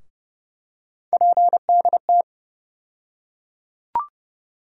Morse code, heard as a keyed tone, about 30 wpm, 700 Hz. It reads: PDT
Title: USA Call Sign Postfixes - 3 Letters 30wpm